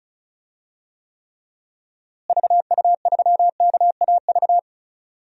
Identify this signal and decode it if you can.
Morse code, heard as a keyed tone, about 36 wpm, 700 Hz. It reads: VU3KAV
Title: DXCC-100 Call Signs - 6 Characters - Repeated 3x - 35wpm